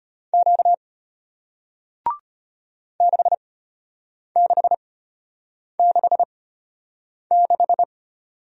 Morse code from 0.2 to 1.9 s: Q

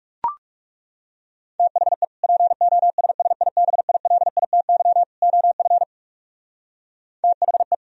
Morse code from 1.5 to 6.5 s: THE POSSIBILITY OF

